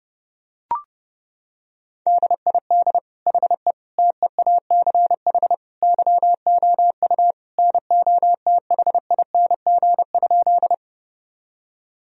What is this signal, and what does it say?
Morse code, heard as a keyed tone, about 30 wpm, 700 Hz. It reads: DID HE TEACH YOU NOTHING?
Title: Sentences from Top 1000 Words - Review Part 2 - 30wpm